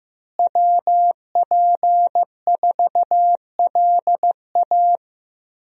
Morse code, heard as a keyed tone, about 15 wpm, 700 Hz. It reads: WP4LA
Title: USA Call Signs - 5 Characters - Repeated 3x - 15wpm